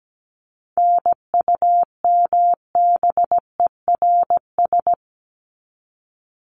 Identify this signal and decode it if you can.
Morse code, heard as a keyed tone, about 17 wpm, 700 Hz. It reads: NUMBERS
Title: ES:S Suffix - Top 100 Words with Base Words 17wpm